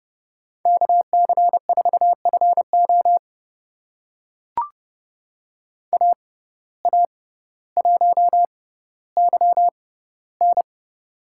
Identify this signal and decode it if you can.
Morse code, heard as a keyed tone, about 30 wpm, spaced as for 15 wpm, 700 Hz. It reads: KC4FO AA1YN
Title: USA Call Signs - 6x Character Spacing - 30wpm